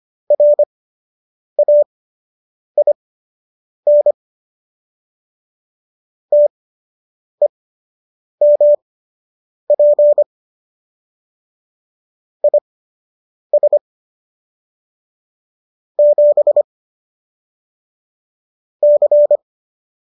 Morse code, heard as a keyed tone, about 25 wpm, spaced as for 8 wpm, 590 Hz. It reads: RAIN TEMP IS 7 C